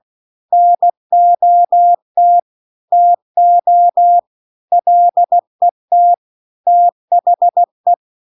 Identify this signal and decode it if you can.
Morse code, heard as a keyed tone, about 16 wpm, 700 Hz. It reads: NOT TO LET THE
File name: CW-The-Big-Twoheaded-River-2-16WPM